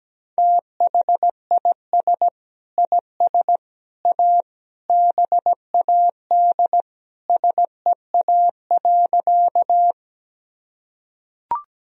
Morse code, heard as a keyed tone, about 17 wpm, 700 Hz. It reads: THIS IS A BAD SEA.
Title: Sentences from Top 1000 Words - Review Part 1 - 17wpm